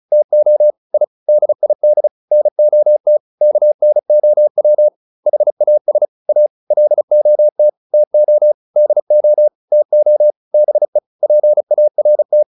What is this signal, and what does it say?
Morse code, heard as a keyed tone, about 34 wpm, 600 Hz. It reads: TO I DID NOT KNOW HAS A LOT TO DO TO BE PART
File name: WD402 35